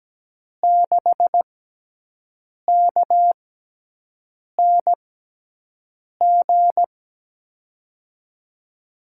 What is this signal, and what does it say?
Morse code, heard as a keyed tone, about 17 wpm, spaced as for 6 wpm, 700 Hz. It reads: 6KNG
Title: USA Call Signs - 6x Character Spacing - 17wpm